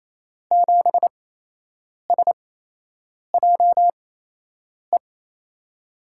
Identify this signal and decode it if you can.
Morse code, heard as a keyed tone, about 28 wpm, spaced as for 8 wpm, 700 Hz. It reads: 7SJE
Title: USA Call Signs - 6 Characters - 8x Character Spacing - 28wpm